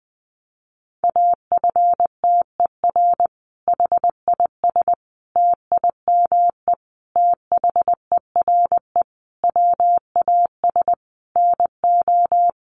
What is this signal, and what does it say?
Morse code, heard as a keyed tone, about 20 wpm, 700 Hz. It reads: AFTER HIS TIME THERE WAS NO
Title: Sentences from Top 1400 Words 20wpm